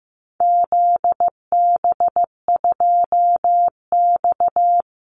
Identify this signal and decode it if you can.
Morse code, heard as a keyed tone, about 15 wpm, 700 Hz. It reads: ZB2X
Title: Call Signs Repeated 2x - 15wpm